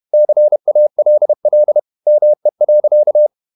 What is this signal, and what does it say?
Morse code, heard as a keyed tone, about 31 wpm, 600 Hz. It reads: CALL ME.
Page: Never 31wpm | CW med Gnister